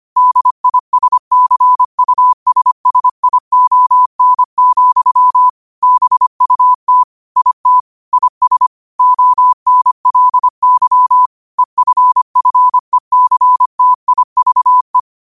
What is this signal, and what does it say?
Morse code, heard as a keyed tone, about 25 wpm, 1000 Hz. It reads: DISCUSSION, BUT IT IS ONLY EFFECTIVE
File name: qotdfiles / 25wpm